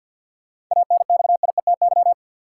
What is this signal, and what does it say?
Morse code, heard as a keyed tone, about 53 wpm, 700 Hz. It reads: ANXIETY
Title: Sentences from Top 1600 Words - New Word Drill Alt - 50wpm